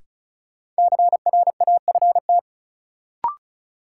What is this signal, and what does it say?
Morse code, heard as a keyed tone, about 36 wpm, 700 Hz. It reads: CRAFT
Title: Sentences from Top 1000 Words - Review Part 6 - New Word Drill - 35wpm